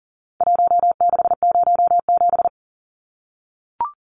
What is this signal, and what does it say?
Morse code, heard as a keyed tone, about 38 wpm, 700 Hz. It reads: 1607
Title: Numbers - Contest Style - 4-Digit 40wpm